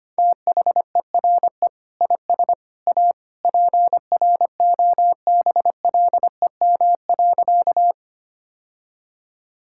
Morse code, heard as a keyed tone, about 25 wpm, 700 Hz. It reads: THERE IS A PROBLEM.
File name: practice / 25wpm / Sentences with 3-Word Phrases - 25wpm